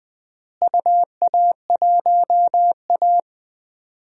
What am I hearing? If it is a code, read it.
Morse code, heard as a keyed tone, about 21 wpm, 700 Hz. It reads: UA1A
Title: DXCC-100 Call Signs - 4 Characters - Repeated 3x - 20wpm